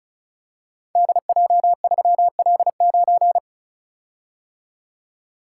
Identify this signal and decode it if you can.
Morse code, heard as a keyed tone, about 36 wpm, 700 Hz. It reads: DJ3L9
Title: Code Groups - 5 Letter:Numbers - 35wpm